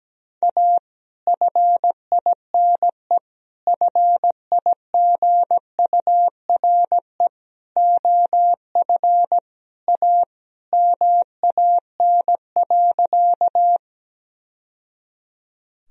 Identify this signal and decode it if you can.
Morse code, heard as a keyed tone, about 17 wpm, 700 Hz. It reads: A FINE FIGURE OF A MAN.
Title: Sentences from Top 1000 Words - Review Part 2 - 17wpm